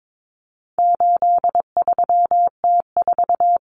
Morse code, heard as a keyed tone, about 22 wpm, 700 Hz. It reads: 83T4
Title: Common Cut Numbers - 4-Digit 22wpm